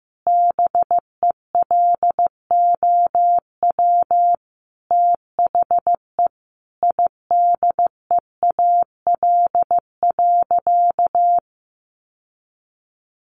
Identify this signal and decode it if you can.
Morse code, heard as a keyed tone, about 14 wpm, 700 Hz. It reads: BELOW THE IDEAL.
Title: Sentences from Top 1400 Words 15wpm